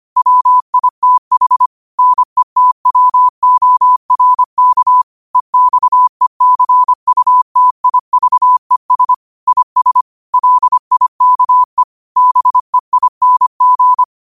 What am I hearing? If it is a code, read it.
Morse code, heard as a keyed tone, about 25 wpm, 1000 Hz. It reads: WITH NETWORK EXECUTIVES IS LIKE BEING